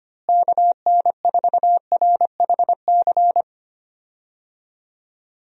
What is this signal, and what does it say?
Morse code, heard as a keyed tone, about 25 wpm, 700 Hz. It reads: KN4RHC
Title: USA Call Signs - 6 Characters - Repeated 3x - 25wpm